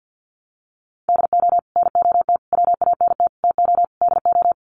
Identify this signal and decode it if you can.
Morse code, heard as a keyed tone, about 50 wpm, 700 Hz. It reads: DO NOT WANT TO DO
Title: Sets of 5 Words 50wpm